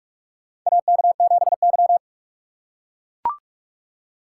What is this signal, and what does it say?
Morse code, heard as a keyed tone, about 44 wpm, 690 Hz. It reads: AK7Y
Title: USA Call Signs - 4 Characters - 2x Character Spacing - 45wpm